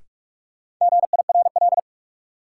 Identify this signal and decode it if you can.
Morse code, heard as a keyed tone, about 46 wpm, 700 Hz. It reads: GIRL
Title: Sentences from Top 400 Words - New Word Drill - 45wpm